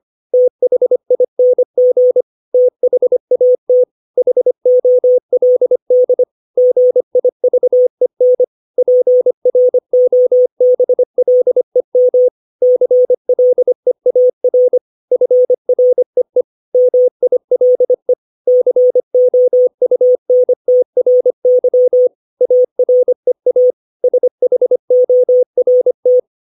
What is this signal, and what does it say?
Morse code, heard as a keyed tone, about 25 wpm, 500 Hz. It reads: THING THAT HOLD GIVEN PROBLEM CLEAR FREE MILE COUNTRY AREA SHORT
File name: 632 words / 632 Words 25WPM